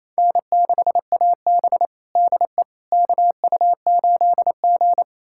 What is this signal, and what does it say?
Morse code, heard as a keyed tone, about 28 wpm, 700 Hz. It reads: N6AB DE KU8G
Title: USA Callsign DE Callsign - 4 Characters - 28wpm